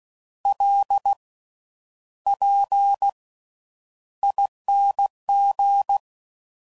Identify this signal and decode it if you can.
Morse code, heard as a keyed tone, about 16 wpm, spaced as for 9 wpm, 800 Hz. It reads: LPING
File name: Slow With Faster Ings